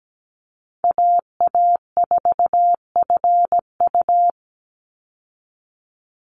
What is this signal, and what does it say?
Morse code, heard as a keyed tone, about 17 wpm, 700 Hz. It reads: AA4FU
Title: Call Signs Repeated 2x - 17wpm